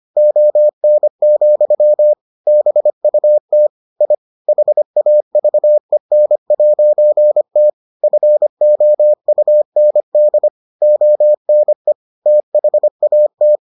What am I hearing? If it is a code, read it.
Morse code, heard as a keyed tone, about 25 wpm, 600 Hz. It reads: ON, BUT I HAVEN'T FOUND ONE THAT